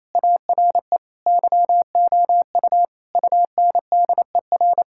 Morse code, heard as a keyed tone, about 27 wpm, 700 Hz. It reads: ARE YOU UNDER
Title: Sentences from Top 1000 Words - Review Part 6 - 28wpm